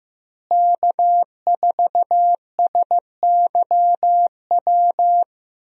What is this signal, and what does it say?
Morse code, heard as a keyed tone, about 15 wpm, 700 Hz. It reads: K4SYW
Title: USA Call Signs - 5 Characters - Repeated 3x - 15wpm